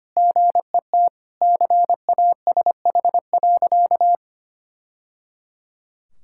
Morse code, heard as a keyed tone, about 25 wpm, 700 Hz. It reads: GET CASH.
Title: Sentences from Top 1000 Words - Review Part 5 - 25wpm